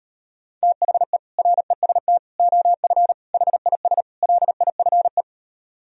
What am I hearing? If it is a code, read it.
Morse code, heard as a keyed tone, about 39 wpm, 700 Hz. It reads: THE REST OF HIS LIFE